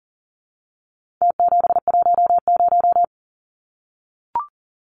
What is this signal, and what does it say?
Morse code, heard as a keyed tone, about 40 wpm, 700 Hz. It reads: T710
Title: Numbers - Contest Style - 4-Digit 40wpm